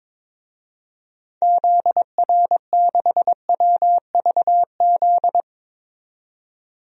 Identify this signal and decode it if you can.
Morse code, heard as a keyed tone, about 22 wpm, 700 Hz. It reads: ZR6WVZ
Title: DXCC-100 Call Signs - 6 Characters - 22wpm